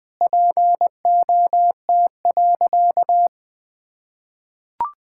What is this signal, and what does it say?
Morse code, heard as a keyed tone, about 20 wpm, 700 Hz. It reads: POT.
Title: Easy Sentences with Hints - 20wpm